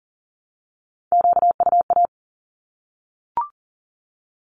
Morse code, heard as a keyed tone, about 40 wpm, 700 Hz. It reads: QUA
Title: Q-Codes Repeated 3x 40wpm